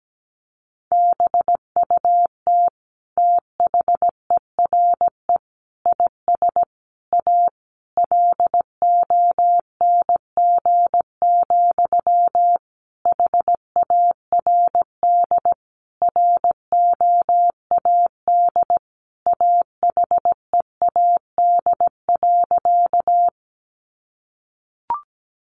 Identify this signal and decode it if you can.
Morse code, heard as a keyed tone, about 17 wpm, 700 Hz. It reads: BUT THERE IS A LONG, HARD ROAD AHEAD.
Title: Sentences from Top 1100 Words 17wpm